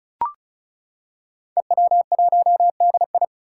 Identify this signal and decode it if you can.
Morse code, heard as a keyed tone, about 36 wpm, 700 Hz. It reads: EW1DI
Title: Five Challenging Letter-Number Combinations 35wpm